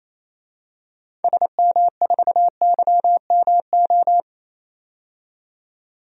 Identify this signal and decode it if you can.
Morse code, heard as a keyed tone, about 28 wpm, 700 Hz. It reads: SM4YMO
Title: DXCC-100 Call Signs - 6 Characters - 28wpm